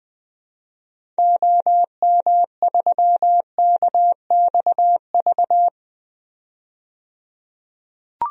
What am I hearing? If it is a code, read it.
Morse code, heard as a keyed tone, about 21 wpm, 700 Hz. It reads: OM3KXV
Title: DXCC-100 Call Signs - 6 Characters - 20wpm